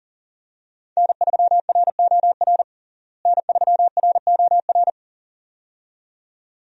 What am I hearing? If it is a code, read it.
Morse code, heard as a keyed tone, about 40 wpm, 700 Hz. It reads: N3ROR N3ROR
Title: USA Call Signs - 5 Characters - Repeated 2x - 40wpm